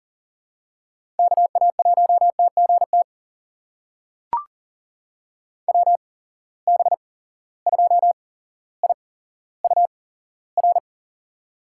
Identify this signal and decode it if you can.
Morse code, heard as a keyed tone, about 40 wpm, spaced as for 17 wpm, 700 Hz. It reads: KA1TGT WB2IUR
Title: USA Call Signs - 8x Character Spacing - 40wpm